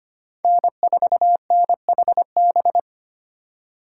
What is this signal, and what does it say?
Morse code, heard as a keyed tone, about 25 wpm, 700 Hz. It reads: N4NHB
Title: USA Call Signs - 5 Characters - 2x Character Spacing - 25wpm